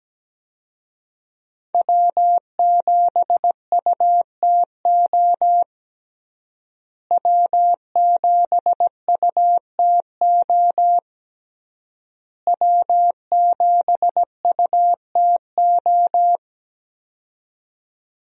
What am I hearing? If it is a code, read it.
Morse code, heard as a keyed tone, about 17 wpm, 700 Hz. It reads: W7UTO W7UTO W7UTO
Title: USA Call Signs - 5 Characters - Repeated 3x - 17wpm